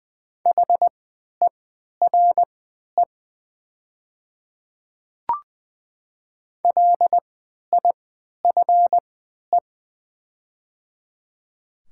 Morse code, heard as a keyed tone, about 20 wpm, spaced as for 11 wpm, 700 Hz. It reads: HERE LIFE
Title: 3x Character Spacing - Top 300 Words - 20wpm